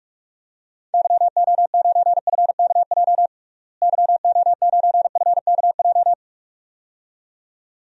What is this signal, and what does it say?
Morse code, heard as a keyed tone, about 43 wpm, 700 Hz. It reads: YO9FKJ YO9FKJ
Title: DXCC-100 Call Signs - 6 Characters - Repeated 2x - 45wpm